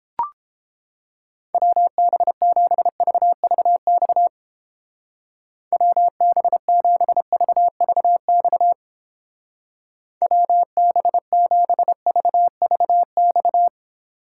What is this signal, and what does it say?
Morse code, heard as a keyed tone, about 29 wpm, 700 Hz. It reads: WB7VVX WB7VVX WB7VVX